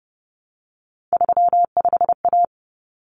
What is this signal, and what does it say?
Morse code, heard as a keyed tone, about 30 wpm, 700 Hz. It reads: 35A
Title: Common Cut Numbers - 3-Digit 30wpm